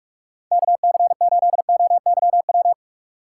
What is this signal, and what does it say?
Morse code, heard as a keyed tone, about 44 wpm, 700 Hz. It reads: KC8OYW